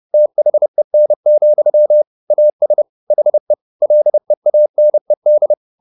Morse code, heard as a keyed tone, about 30 wpm, 600 Hz. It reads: THEN, AS HE LEANED
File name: the old man and the sea / the old man and the sea 1 30wpm